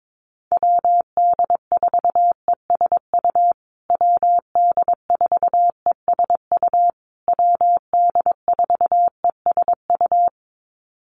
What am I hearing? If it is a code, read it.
Morse code, heard as a keyed tone, about 22 wpm, 700 Hz. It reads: WD4ESU WD4ESU WD4ESU
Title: USA Call Signs Repeated 3x 22wpm